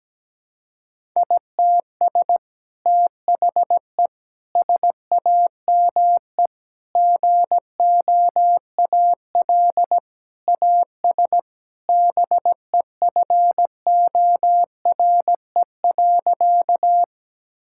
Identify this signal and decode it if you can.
Morse code, heard as a keyed tone, about 17 wpm, 700 Hz. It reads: ITS THE SAME GOAL AS BEFORE.
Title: Sentences from Top 1000 Words - Review Part 5 - 17wpm